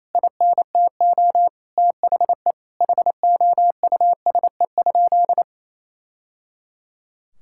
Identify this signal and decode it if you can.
Morse code, heard as a keyed tone, about 28 wpm, 700 Hz. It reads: INTO THE HOUSE?
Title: Sentences from Top 1000 Words - Review Part 5 - 28wpm